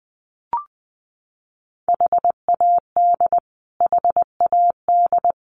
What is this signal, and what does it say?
Morse code, heard as a keyed tone, about 20 wpm, 700 Hz. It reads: HAD HAD
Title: Top 200 Words Repeated 20wpm